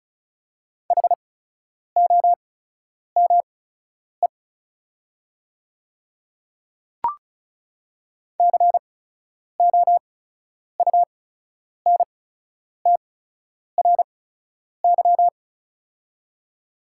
Morse code, heard as a keyed tone, about 34 wpm, spaced as for 10 wpm, 700 Hz. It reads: HOME COUNTRY
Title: 8x Character Spacing - Top 300 Words - 35wpm